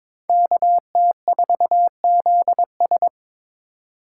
Morse code, heard as a keyed tone, about 26 wpm, 700 Hz. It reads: KT4ZS
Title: Speed Racing - USA 2x2 Call Signs 22wpm